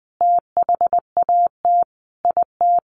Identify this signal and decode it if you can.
Morse code, heard as a keyed tone, about 20 wpm, 700 Hz. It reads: THAT IT
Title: Sentences from Top 1500 Words 20wpm